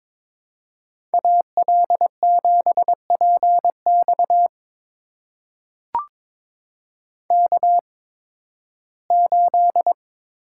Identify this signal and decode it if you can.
Morse code, heard as a keyed tone, about 22 wpm, spaced as for 14 wpm, 700 Hz. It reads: AL7PX K8
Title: USA Call Signs - 8x Character Spacing - 22wpm